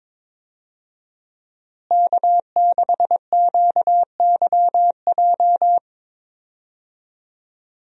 Morse code, heard as a keyed tone, about 22 wpm, 700 Hz. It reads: K6QYJ
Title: USA Call Signs - 5 Characters - Repeated 3x - 22wpm